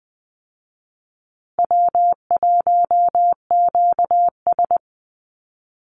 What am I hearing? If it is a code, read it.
Morse code, heard as a keyed tone, about 20 wpm, 700 Hz. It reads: W1QS
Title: Easy Call Signs Repeated 3x - 20wpm